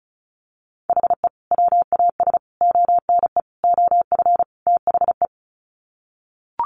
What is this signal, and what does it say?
Morse code, heard as a keyed tone, about 34 wpm, 700 Hz. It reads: HE WAS ONE OF THE
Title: Sets of 5 Words 35wpm